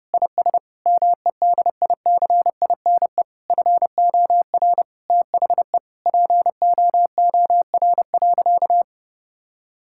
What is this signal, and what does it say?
Morse code, heard as a keyed tone, about 30 wpm, 700 Hz. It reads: IS MEDICINE FOR THE POOR.
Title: Sentences from Top 1000 Words - Review Part 6 - 30wpm